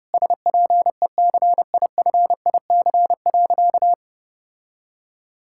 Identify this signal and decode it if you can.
Morse code, heard as a keyed tone, about 30 wpm, 700 Hz. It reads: SPECIFIC.
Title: Sentences from Top 1000 Words - Review Part 6 - 30wpm